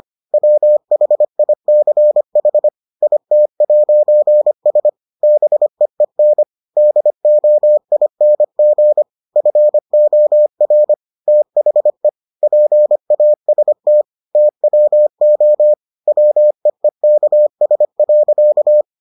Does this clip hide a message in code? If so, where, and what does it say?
Yes: Morse code from 0.3 to 18.8 s, heard as a keyed tone, about 25 wpm, 600 Hz. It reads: WHICH IT'S BEEN DOING FOR THE PAST TWO WEEKS.